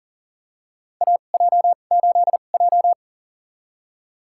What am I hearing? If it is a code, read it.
Morse code, heard as a keyed tone, about 40 wpm, spaced as for 29 wpm, 700 Hz. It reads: AJ8J
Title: USA Call Signs - 4 Characters - 2x Character Spacing - 40wpm